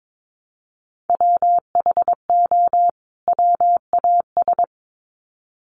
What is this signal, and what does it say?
Morse code, heard as a keyed tone, about 22 wpm, 700 Hz. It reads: WHO WAS
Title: Sets of 2 Words 22wpm